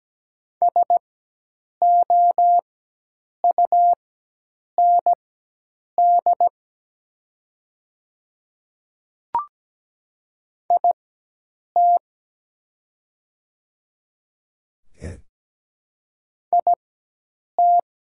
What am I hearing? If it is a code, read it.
Morse code, heard as a keyed tone, about 17 wpm, spaced as for 8 wpm, 700 Hz. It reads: SOUND IT IT
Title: 4x Character Spacing - Top 100 Words - 17wpm